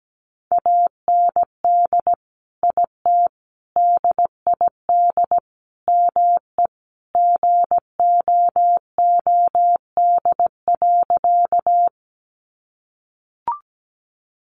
Morse code, heard as a keyed tone, about 17 wpm, 700 Hz. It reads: AND IT DID ME GOOD.
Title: Sentences from Top 200 Words 17wpm